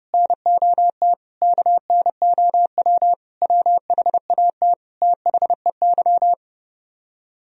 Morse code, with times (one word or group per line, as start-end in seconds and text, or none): 0.1-1.1 NOT
1.4-3.1 KNOW
3.4-4.7 WHAT
5.0-6.3 THEY